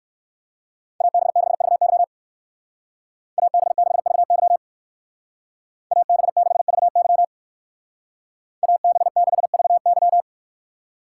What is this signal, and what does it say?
Morse code, heard as a keyed tone, about 56 wpm, 700 Hz. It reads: AB6VY AB6VY AB6VY AB6VY
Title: Speed Racing - USA 2x2 Call Signs 45wpm